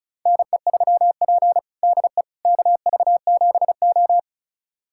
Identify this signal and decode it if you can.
Morse code, heard as a keyed tone, about 36 wpm, 700 Hz. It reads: NE3P DE KV7O
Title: USA Callsign DE Callsign - 4 Characters - 35wpm